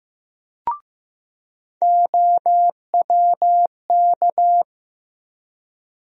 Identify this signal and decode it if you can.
Morse code, heard as a keyed tone, about 15 wpm, 700 Hz. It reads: OWK
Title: USA Call Sign Postfixes - 3 Letters 15wpm